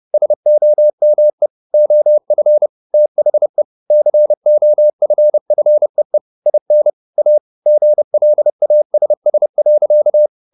Morse code, heard as a keyed tone, about 30 wpm, 600 Hz. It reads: SOME OF THE COFFEE IN A GLASS.